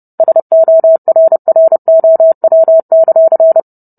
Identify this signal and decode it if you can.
Morse code, heard as a keyed tone, about 30 wpm, 650 Hz. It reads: SORROW;